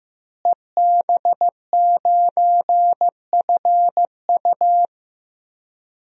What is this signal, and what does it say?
Morse code, heard as a keyed tone, about 16 wpm, 700 Hz. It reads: EB9FU
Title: DXCC-100 Call Signs - 5 Characters - Repeated 3x - 15wpm